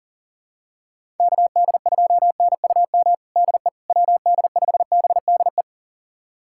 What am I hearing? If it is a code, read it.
Morse code, heard as a keyed tone, about 40 wpm, 700 Hz. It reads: KD2NUM DE WD5BDE
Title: USA Callsign DE Callsign 40wpm